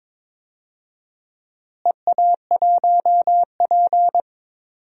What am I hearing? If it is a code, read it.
Morse code, heard as a keyed tone, about 21 wpm, 700 Hz. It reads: EA1P
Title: DXCC-100 Call Signs - 4 Characters - Repeated 3x - 22wpm